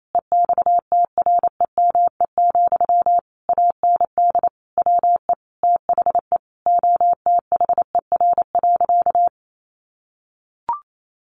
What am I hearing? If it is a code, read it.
Morse code, heard as a keyed tone, about 28 wpm, 700 Hz. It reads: EXTREME, AND WE THE OTHER.
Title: Sentences from Top 1300 Words 28wpm